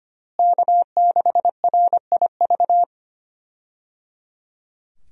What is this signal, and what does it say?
Morse code, heard as a keyed tone, about 25 wpm, 700 Hz. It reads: K6RIV
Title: USA Call Signs - 5 Characters - Repeated 3x - 25wpm